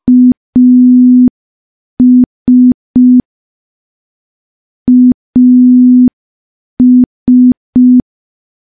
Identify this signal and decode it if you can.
Morse code, heard as a keyed tone, about 5 wpm, 250 Hz. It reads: AS AS
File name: morse / morse-training-05WPM